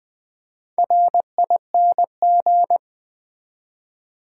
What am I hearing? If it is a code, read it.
Morse code, heard as a keyed tone, about 20 wpm, 700 Hz. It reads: RING
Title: Easy Words with Hints - 20wpm